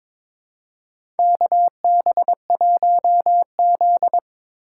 Morse code, heard as a keyed tone, about 23 wpm, 700 Hz. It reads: KB1Z
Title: USA Call Signs - 4 Characters - 2x Character Spacing - 22wpm